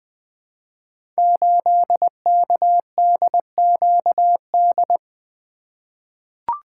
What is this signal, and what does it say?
Morse code, heard as a keyed tone, about 20 wpm, 700 Hz. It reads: 8KDQD
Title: Code Groups - 5 Letter:Numbers - 20wpm